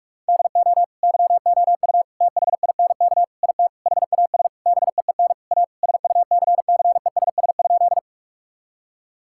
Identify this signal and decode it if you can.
Morse code, heard as a keyed tone, about 47 wpm, 700 Hz. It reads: DO YOU THINK IT HAS BEEN A SUCCESS?
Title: Sentences from Top 1000 Words - Review Part 6 - 45wpm